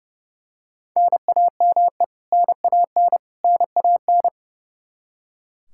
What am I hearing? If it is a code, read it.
Morse code, heard as a keyed tone, about 32 wpm, 700 Hz. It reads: NAME NAN NAN
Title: Practice QSO - Exchange 1 - Beginning - 30wpm